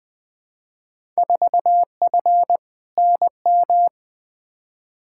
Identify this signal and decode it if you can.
Morse code, heard as a keyed tone, about 20 wpm, 700 Hz. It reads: 4F NM
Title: Field Day Exchanges Encore - 20wpm